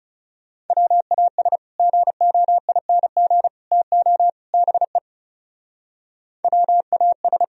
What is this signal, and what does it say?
Morse code, heard as a keyed tone, about 37 wpm, 700 Hz. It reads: WAS GOING TO BE WAS
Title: Speed-Racing - Sets of 4 Words 30wpm